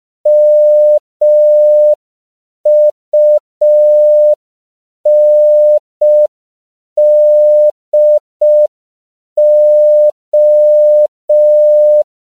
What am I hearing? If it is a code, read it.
Morse code, heard as a keyed tone, about 5 wpm, 600 Hz. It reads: MUNDO
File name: CW / 5PPM